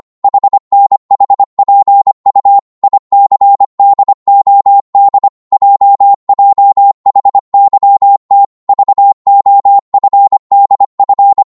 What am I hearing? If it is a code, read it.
Morse code, heard as a keyed tone, about 25 wpm, 800 Hz. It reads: HNHPU ICDOD JJHYT VOFDF
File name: Morse / ltr-25wpm